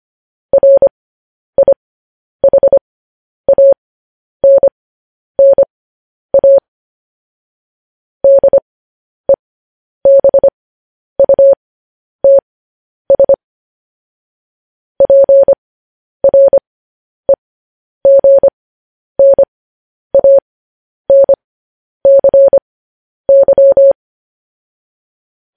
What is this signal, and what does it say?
Morse code, heard as a keyed tone, about 25 wpm, spaced as for 10 wpm, 550 Hz. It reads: RIHANNA DEBUTS PREGNANCY
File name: podcast / News headlines-1x-25x10